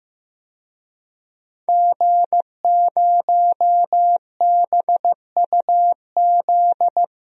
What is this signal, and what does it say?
Morse code, heard as a keyed tone, about 15 wpm, 700 Hz. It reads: G0BUZ
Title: DXCC-100 Call Signs - 5 Characters - 15wpm